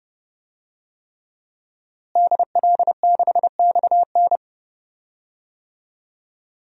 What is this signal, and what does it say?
Morse code, heard as a keyed tone, about 30 wpm, 700 Hz. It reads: DL6XN
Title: DXCC-100 Call Signs - 5 Characters - Repeated 3x - 30wpm